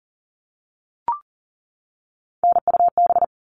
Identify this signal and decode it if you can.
Morse code, heard as a keyed tone, about 40 wpm, 700 Hz. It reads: NUB
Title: All Cut Numbers - 3-Digit 40wpm